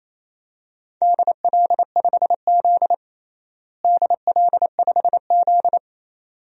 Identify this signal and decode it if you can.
Morse code, heard as a keyed tone, about 28 wpm, 700 Hz. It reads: DL5Z DL5Z
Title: DXCC-100 Call Signs - 4 Characters - Repeated 2x - 28wpm